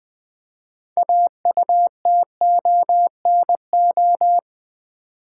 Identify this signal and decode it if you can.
Morse code, heard as a keyed tone, about 20 wpm, 700 Hz. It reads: AUTONO
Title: Build A Word - 3 Letters At A Time - 8-Letter Words - 20wpm